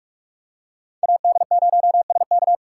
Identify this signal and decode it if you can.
Morse code, heard as a keyed tone, about 44 wpm, 700 Hz. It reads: AD0SK
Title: USA Call Signs - 5 Characters - 6x Character Spacing - 45wpm